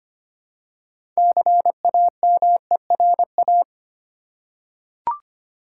Morse code, heard as a keyed tone, about 25 wpm, 700 Hz. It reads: CAMERA
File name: practice / 25wpm / Easy Words with Hints - 25wpm